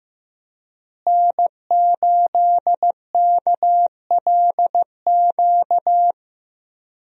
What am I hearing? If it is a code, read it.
Morse code, heard as a keyed tone, about 15 wpm, 700 Hz. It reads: N8KLQ
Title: USA Call Signs - 5 Characters - Repeated 3x - 15wpm